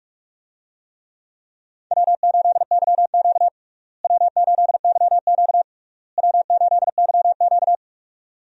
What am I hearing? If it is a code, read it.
Morse code, heard as a keyed tone, about 46 wpm, 700 Hz. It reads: W8YQ W8YQ W8YQ
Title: USA Call Signs - 4 Characters - Repeated 3x - 45wpm